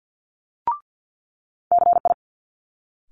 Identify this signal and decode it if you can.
Morse code, heard as a keyed tone, about 50 wpm, 700 Hz. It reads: CI
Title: Multi-letter Phonemes Quick Fire 50wpm